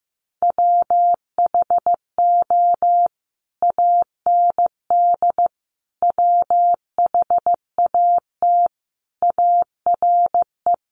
Morse code, heard as a keyed tone, about 15 wpm, 700 Hz. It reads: WHO AND WHAT ARE
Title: Sentences from Top 100 Words 15wpm